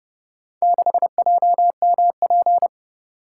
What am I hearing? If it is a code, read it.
Morse code, heard as a keyed tone, about 29 wpm, 700 Hz. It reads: 6JMP